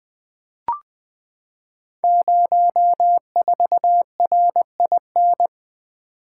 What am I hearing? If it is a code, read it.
Morse code, heard as a keyed tone, about 20 wpm, 700 Hz. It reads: 04RIN
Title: Code Groups - 5 Letter:Numbers - 20wpm